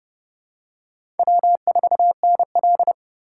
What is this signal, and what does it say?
Morse code, heard as a keyed tone, about 30 wpm, 700 Hz. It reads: W4NL
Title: USA Call Signs - 4x Character Spacing - 30wpm